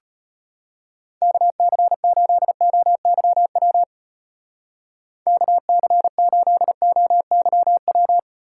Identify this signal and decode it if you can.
Morse code, heard as a keyed tone, about 32 wpm, 700 Hz. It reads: KC8OYW KC8OYW